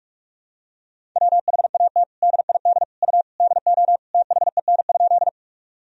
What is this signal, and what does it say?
Morse code, heard as a keyed tone, about 47 wpm, 700 Hz. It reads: WHAT DID U DO THEN?
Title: Sentences from Top 100 Words and Abbreviations 45wpm